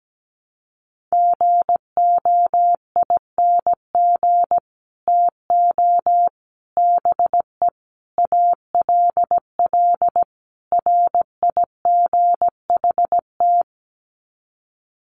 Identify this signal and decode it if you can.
Morse code, heard as a keyed tone, about 17 wpm, 700 Hz. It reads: GOING TO BE ALL RIGHT
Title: Sets of 5 Words Encore2 17wpm